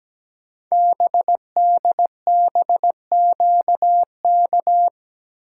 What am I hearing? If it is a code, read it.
Morse code, heard as a keyed tone, about 17 wpm, 700 Hz. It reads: BDBQK
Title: Code Groups - 5 Letters - 17wpm